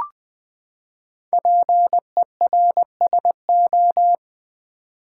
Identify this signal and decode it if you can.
Morse code, heard as a keyed tone, about 20 wpm, 700 Hz. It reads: PERSO
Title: Build A Word - 5 Letters At A Time - 10-Letter Words - 20wpm